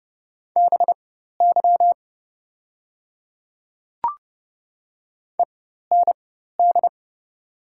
Morse code, heard as a keyed tone, about 30 wpm, spaced as for 14 wpm, 700 Hz. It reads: BY END
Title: 4x Character Spacing - Top 300 Words - 30wpm